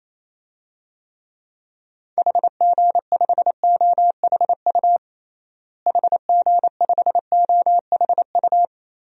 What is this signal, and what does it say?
Morse code, heard as a keyed tone, about 28 wpm, 700 Hz. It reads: HG5OHU HG5OHU
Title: DXCC-100 Call Signs - 6 Characters - Repeated 3x - 28wpm